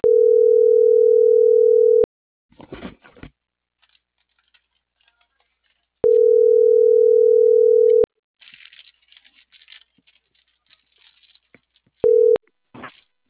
Ringback tone (North America), 440 + 480 Hz.